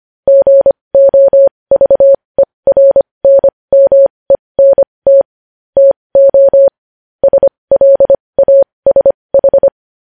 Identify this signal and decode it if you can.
Morse code, heard as a keyed tone, about 25 wpm, spaced as for 20 wpm, 550 Hz. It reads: GOVERNMENT TO SLASH
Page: news-headlines-in-morse-code-at-20-wpm